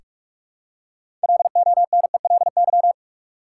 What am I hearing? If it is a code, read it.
Morse code, heard as a keyed tone, about 47 wpm, 700 Hz. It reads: LONELY
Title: Six Letter Words - 45wpm